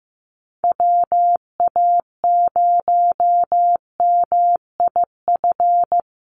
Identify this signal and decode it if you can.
Morse code, heard as a keyed tone, about 14 wpm, 700 Hz. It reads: WA0MIF
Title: USA Call Signs 15wpm